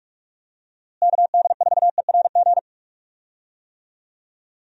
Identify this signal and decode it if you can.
Morse code, heard as a keyed tone, about 41 wpm, 700 Hz. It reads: KD4ERG
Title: USA Call Signs - 6 Characters - Repeated 2x - 45wpm